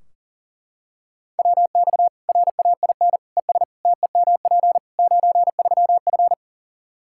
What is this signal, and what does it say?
Morse code, heard as a keyed tone, about 37 wpm, 700 Hz. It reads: WX RAIN ES TEMP 93F
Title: Practice QSO - Exchange 2 - Info - 40wpm